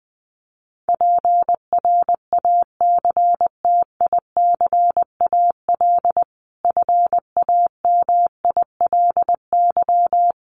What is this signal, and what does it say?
Morse code, heard as a keyed tone, about 20 wpm, 700 Hz. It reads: PRACTICAL FAMILY
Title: Sentences from Top 1400 Words 20wpm